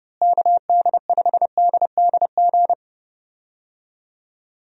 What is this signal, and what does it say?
Morse code, heard as a keyed tone, about 30 wpm, 700 Hz. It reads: KD5DDG
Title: USA Call Signs - 6 Characters - Repeated 2x - 30wpm